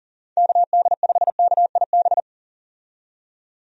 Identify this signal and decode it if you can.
Morse code, heard as a keyed tone, about 37 wpm, 700 Hz. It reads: KD5KIB
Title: USA Call Signs - 6 Characters - 40wpm